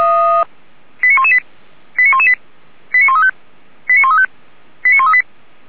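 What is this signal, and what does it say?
Five-tone selective call (ZVEI), heard as quick runs of five tones, tones of about 80 ms. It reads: 89118 89118 89135 89135 89138
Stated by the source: ZVEI five-tone selective call received on an SDR.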